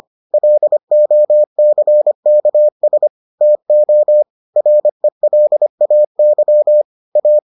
Morse code, heard as a keyed tone, about 25 wpm, 600 Hz. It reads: LOCKS TO RELAY A